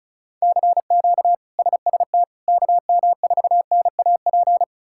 Morse code, heard as a keyed tone, about 36 wpm, 700 Hz. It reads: CQ SST KM4NAP
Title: K1USN SST QSO Practice - 35wpm